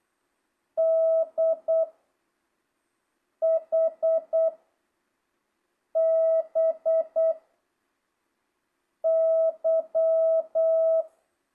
Morse code, heard as a keyed tone, about 8 wpm, 660 Hz. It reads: D H B Y
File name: jh8cbh / 2025cw